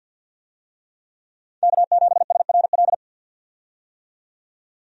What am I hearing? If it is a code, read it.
Morse code, heard as a keyed tone, about 48 wpm, 700 Hz. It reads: K7SRL
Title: USA Call Signs - 5 Characters - 50wpm